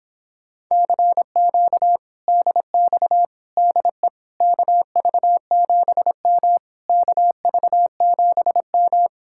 Morse code, heard as a keyed tone, about 29 wpm, 700 Hz. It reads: CQ DX DE KV7M KV7M